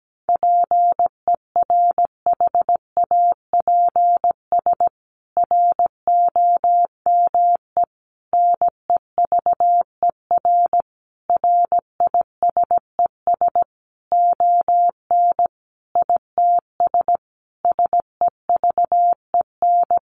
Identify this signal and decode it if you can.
Morse code, heard as a keyed tone, about 17 wpm, 700 Hz. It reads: PERHAPS ROME NEVER RISES ON ITS SEVEN
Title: A Sound Of Thunder 17wpm